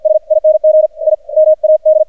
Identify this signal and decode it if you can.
Morse code, heard as a keyed tone, about 46 wpm, 620 Hz. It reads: SING I MAD
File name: archived / CRSnet